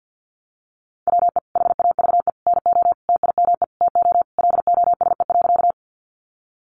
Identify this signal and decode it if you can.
Morse code, heard as a keyed tone, about 50 wpm, 700 Hz. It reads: WE HAVE NO TIME TO LOSE.
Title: Sentences from Top 1100 Words 50wpm